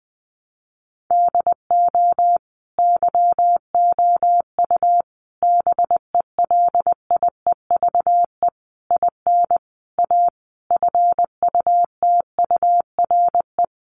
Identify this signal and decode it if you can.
Morse code, heard as a keyed tone, about 20 wpm, 700 Hz. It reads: DO YOU BELIEVE IN A FUTURE
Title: Sentences from Top 1000 Words 20wpm